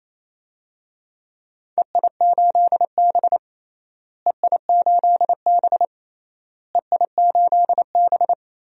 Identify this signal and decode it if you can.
Morse code, heard as a keyed tone, about 28 wpm, 700 Hz. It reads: EI8B EI8B EI8B
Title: DXCC-100 Call Signs - 4 Characters - Repeated 3x - 28wpm